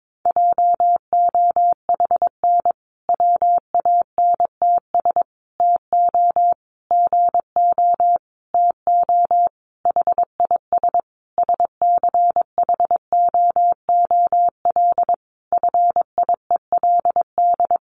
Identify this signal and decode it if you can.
Morse code, heard as a keyed tone, about 22 wpm, 700 Hz. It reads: JOHN WANTS TO GO TO HIS SCHOOL FIELD
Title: Additional Easy Phrases 22wpm